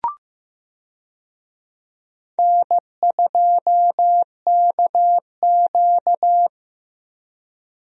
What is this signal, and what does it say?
Morse code, heard as a keyed tone, about 15 wpm, 700 Hz. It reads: N2KQ
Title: USA Call Signs - 4 Characters - Repeated 2x - 15wpm